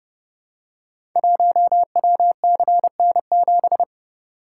Morse code, heard as a keyed tone, about 31 wpm, 700 Hz. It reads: 1WCN7
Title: Code Groups - 5 Letter:Numbers - 30wpm